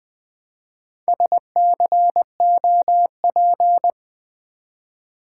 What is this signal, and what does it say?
Morse code, heard as a keyed tone, about 20 wpm, 700 Hz. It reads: SCOP
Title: Build A Word - 1 Letter At A Time - 5-Letter Words - 20wpm